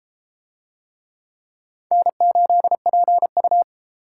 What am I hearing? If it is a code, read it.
Morse code, heard as a keyed tone, about 31 wpm, 700 Hz. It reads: N8PU